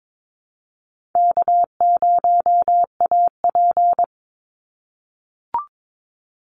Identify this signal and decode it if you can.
Morse code, heard as a keyed tone, about 22 wpm, 700 Hz. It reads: K0AP
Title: Easy Call Signs - 22wpm